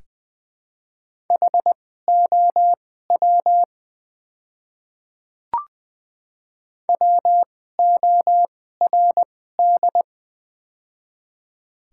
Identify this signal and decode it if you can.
Morse code, heard as a keyed tone, about 21 wpm, spaced as for 15 wpm, 700 Hz. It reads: HOW WORD
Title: 2x Character Spacing - Top 100 Words - 20wpm